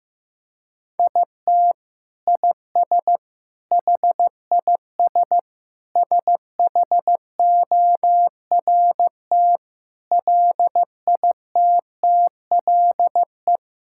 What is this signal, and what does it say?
Morse code, heard as a keyed tone, about 15 wpm, 700 Hz. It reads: IT IS HIS SHORT LITTLE
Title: Sentences from Top 1600 Words 15wpm